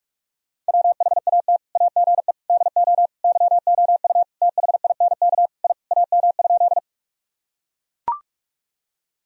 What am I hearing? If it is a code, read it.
Morse code, heard as a keyed tone, about 42 wpm, 700 Hz. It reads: WHAT AGE DO YOU THINK I AM?
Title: Sentences from Top 1000 Words - Review Part 3 - 45wpm